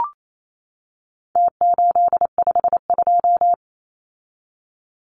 Morse code, heard as a keyed tone, about 28 wpm, 700 Hz. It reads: T852
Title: Numbers - Contest Style - 4-Digit 28wpm